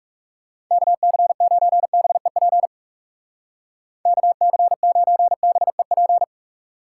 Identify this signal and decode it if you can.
Morse code, heard as a keyed tone, about 37 wpm, 700 Hz. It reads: KC9BEP KC9BEP
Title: Speed Racing - USA Call Signs 30wpm